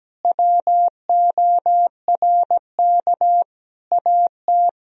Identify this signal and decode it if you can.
Morse code, heard as a keyed tone, about 17 wpm, 700 Hz. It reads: WORK AT
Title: Sentences from Top 1000 Words - Review Part 1 - 17wpm